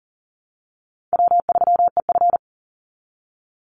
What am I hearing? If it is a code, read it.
Morse code, heard as a keyed tone, about 40 wpm, 700 Hz. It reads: W3EF
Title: Easy Call Signs Repeated 2x - 40wpm